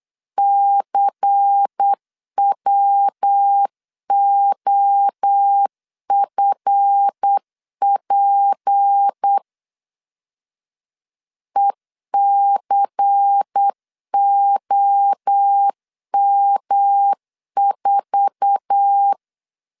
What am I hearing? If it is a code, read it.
Morse code, heard as a keyed tone, about 8 wpm, 800 Hz. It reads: CWOFP ECOM4
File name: LEZIONI / 8 YJ45 / 8 YJ4 8wpm 3246